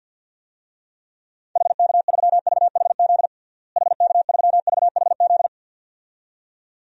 Morse code, heard as a keyed tone, about 50 wpm, 700 Hz. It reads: HK3VHZ HK3VHZ
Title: DXCC-100 Call Signs - 6 Characters - Repeated 2x - 50wpm